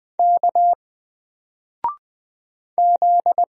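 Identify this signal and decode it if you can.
Morse code, heard as a keyed tone, about 20 wpm, 700 Hz. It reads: K Z
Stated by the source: Single letter and number Morse drill at 20 wpm, from N4LCD.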